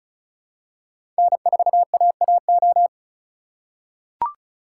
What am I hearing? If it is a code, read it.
Morse code, heard as a keyed tone, about 34 wpm, 700 Hz. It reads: N4AAO
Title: USA Call Signs - 5 Characters - 3x Character Spacing - 35wpm